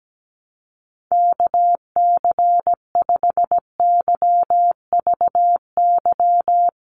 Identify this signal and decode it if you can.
Morse code, heard as a keyed tone, about 17 wpm, 700 Hz. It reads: KC5YVY
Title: USA Call Signs 17wpm